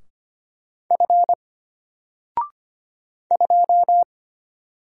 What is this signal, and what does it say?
Morse code, heard as a keyed tone, about 25 wpm, 700 Hz. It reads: F 2
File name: Single-Letter-Number-25wpm